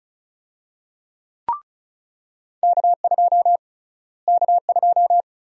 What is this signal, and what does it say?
Morse code, heard as a keyed tone, about 36 wpm, 700 Hz. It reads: K2 K2
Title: USA Call Sign Prefixes - Repeated 2x - 35wpm